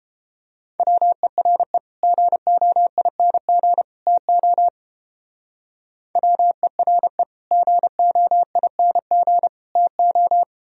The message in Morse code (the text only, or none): WERE GOING TO WERE GOING TO